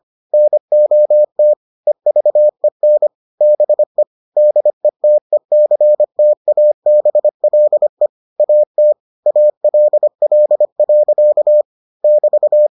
Morse code, heard as a keyed tone, about 25 wpm, 600 Hz. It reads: NOT EVEN BE DETECTABLE AT ALL. <BT>